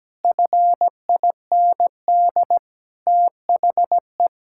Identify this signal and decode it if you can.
Morse code, heard as a keyed tone, about 17 wpm, 700 Hz. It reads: FIND THE
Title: Sentences from Top 1000 Words - Review Part 6 - 17wpm